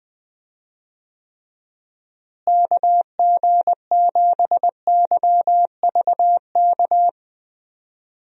Morse code, heard as a keyed tone, about 21 wpm, 700 Hz. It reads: KG7YVK
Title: USA Call Signs - 6 Characters - Repeated 2x - 20wpm